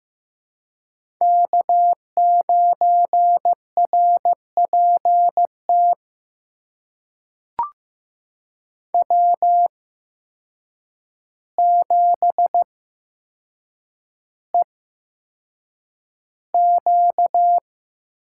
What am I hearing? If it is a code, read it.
Morse code, heard as a keyed tone, about 15 wpm, spaced as for 7 wpm, 700 Hz. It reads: K9RPT W7EQ
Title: USA Call Signs - 8x Character Spacing - 15wpm